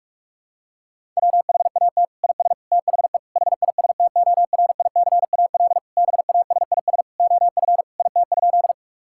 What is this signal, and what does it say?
Morse code, heard as a keyed tone, about 41 wpm, 700 Hz. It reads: WHAT IS THE HISTORICAL BASIS OF IT?
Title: Sentences from Top 1000 Words - Review Part 5 - 45wpm